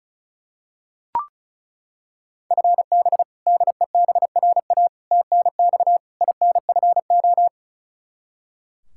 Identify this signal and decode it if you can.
Morse code, heard as a keyed tone, about 34 wpm, 700 Hz. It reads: FB DEBRA TNX INFO
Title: Practice QSO - Exchange 3 - More Info - 35wpm